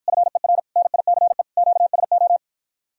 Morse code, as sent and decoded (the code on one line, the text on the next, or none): ..-. .-. / -. .. -.-. / --.- ... ---
FER NICE QSO